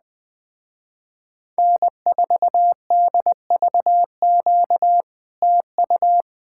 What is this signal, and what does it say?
Morse code, heard as a keyed tone, about 20 wpm, 700 Hz. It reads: N4DVQ TU
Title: Speed-Racing Common QSO Phrases Encore - 17wpm